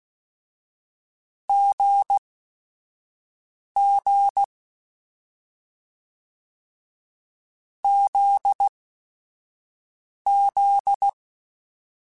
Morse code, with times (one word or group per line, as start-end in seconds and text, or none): 1.5-4.4 GG
7.8-11.1 ZZ